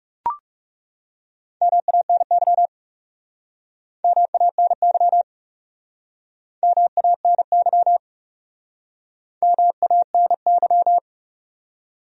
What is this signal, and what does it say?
Morse code, heard as a keyed tone, about 37 wpm, 700 Hz. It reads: MANY MANY MANY MANY
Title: Speed-Racing Top 100 Words 30wpm